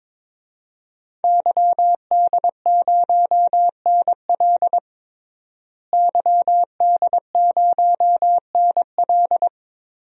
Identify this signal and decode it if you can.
Morse code, heard as a keyed tone, about 22 wpm, 700 Hz. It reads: YD0NL YD0NL
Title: DXCC-100 Call Signs - 5 Characters - Repeated 2x - 22wpm